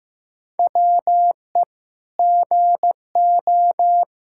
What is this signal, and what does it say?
Morse code, heard as a keyed tone, about 16 wpm, 700 Hz. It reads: WE GO
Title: Sentences from Top 1000 Words - Review Part 5 - 15wpm